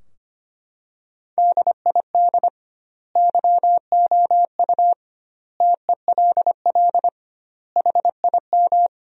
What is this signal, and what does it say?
Morse code, heard as a keyed tone, about 25 wpm, 700 Hz. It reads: DID YOU TELL HIM
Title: 2x Word-Spacing - Sentences from Top 400 Words - 25wpm